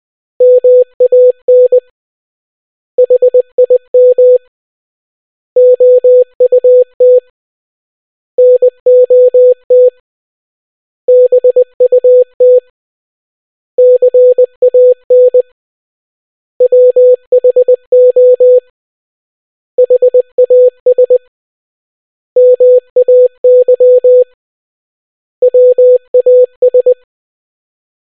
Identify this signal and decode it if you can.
Morse code, heard as a keyed tone, about 20 wpm, 500 Hz. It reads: MAN HIM OUT NOT BUT CAN WHO HAS MAY WAS